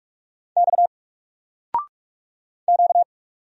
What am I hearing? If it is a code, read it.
Morse code, heard as a keyed tone, about 44 wpm, 700 Hz. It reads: X Q
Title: Single Character (Letter:Number:Punctuation:Prosign) 45wpm